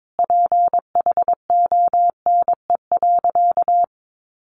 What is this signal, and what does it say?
Morse code, heard as a keyed tone, about 22 wpm, 700 Hz. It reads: PHONE.
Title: Sentences from Top 1100 Words 22wpm